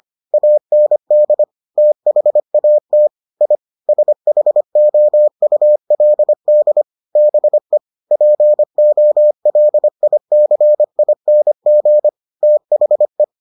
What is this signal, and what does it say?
Morse code, heard as a keyed tone, about 25 wpm, 600 Hz. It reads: AND THAT I SHOULD BE POLICING THE